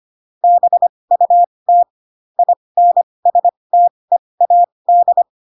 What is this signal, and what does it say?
Morse code, heard as a keyed tone, about 25 wpm, spaced as for 20 wpm, 700 Hz. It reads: BUT INSTEAD